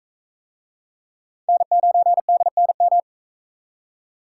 Morse code, heard as a keyed tone, about 39 wpm, 700 Hz. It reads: N9DNM